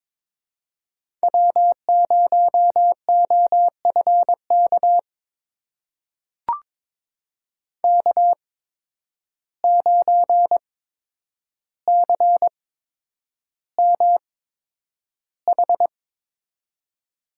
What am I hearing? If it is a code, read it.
Morse code, heard as a keyed tone, about 21 wpm, spaced as for 9 wpm, 700 Hz. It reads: W0OFK K9CMH
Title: USA Call Signs - 5 Characters - 8x Character Spacing - 22wpm